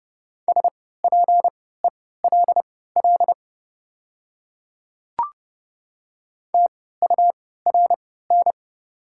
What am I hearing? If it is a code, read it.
Morse code, heard as a keyed tone, about 30 wpm, spaced as for 17 wpm, 700 Hz. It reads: SPELL TURN
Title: 3x Character Spacing - Top 300 Words - 30wpm